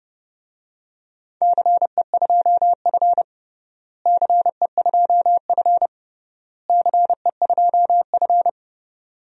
Morse code, heard as a keyed tone, about 30 wpm, 700 Hz. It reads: CE2F CE2F CE2F
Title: DXCC-100 Call Signs - 4 Characters - Repeated 3x - 30wpm